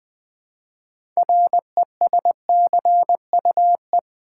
Morse code, heard as a keyed tone, about 20 wpm, 700 Hz. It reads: RESCUE
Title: Build A Word - 3 Letters At A Time - 6-Letter Words - 20wpm